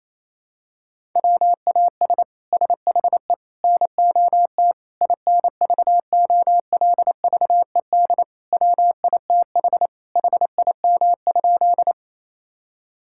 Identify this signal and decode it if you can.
Morse code, heard as a keyed tone, about 27 wpm, 700 Hz. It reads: WAS SHE NOT INVOLVED WITH HIM?
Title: Sentences from Top 1000 Words - Review Part 5 - 28wpm